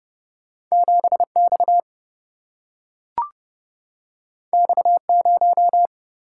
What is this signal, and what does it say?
Morse code, heard as a keyed tone, about 30 wpm, 700 Hz. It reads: 7 X X 0
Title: Single-Letter Single-Number 30wpm